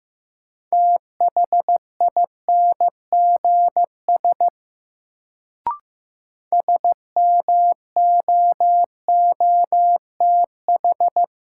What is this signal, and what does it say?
Morse code, heard as a keyed tone, about 15 wpm, 700 Hz. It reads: THINGS SMOOTH
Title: Sentences from Top 1400 Words - New Word Drill Alt - 15wpm